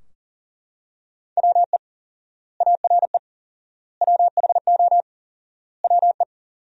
Morse code, heard as a keyed tone, about 40 wpm, 700 Hz. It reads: WE ARE WHO WE
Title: 4x Word-Spacing - Sentences from Top 200 Words - 40wpm